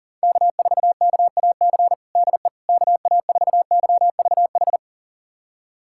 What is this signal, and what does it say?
Morse code, heard as a keyed tone, about 40 wpm, 700 Hz. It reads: K4KAC DE KA4YVH
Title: USA Callsign DE Callsign 40wpm